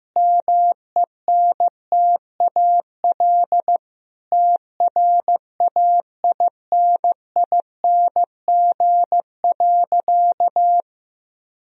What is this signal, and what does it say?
Morse code, heard as a keyed tone, about 15 wpm, 700 Hz. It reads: MENTAL TRAINING.
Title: Sentences from Top 1000 Words - Review Part 4 - 15wpm